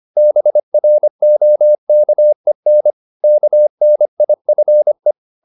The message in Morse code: BROKEN KNIFE

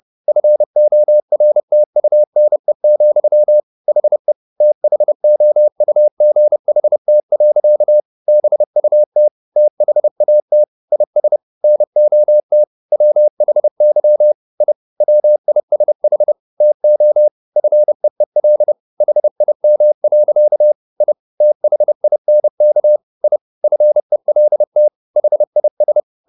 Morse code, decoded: FORTUNE, HE THOUGHT. BUT THAT IS NOT WHY I WISH TO FEEL HIM. I THINK I FELT HIS

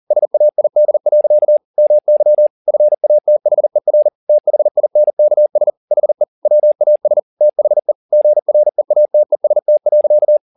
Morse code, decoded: SAID. MY FATHER THINKS HE WAS THE GREATEST.

